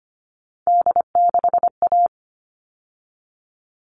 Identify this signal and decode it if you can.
Morse code, heard as a keyed tone, about 25 wpm, 700 Hz. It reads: D6A
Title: All Cut Numbers - 3-Digit 25wpm